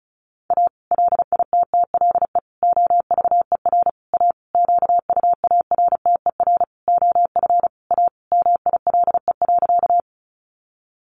Morse code, heard as a keyed tone, about 34 wpm, 700 Hz. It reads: A LITTLE OVER A QUARTER OF A MILE.
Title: Sentences from Top 1300 Words 35wpm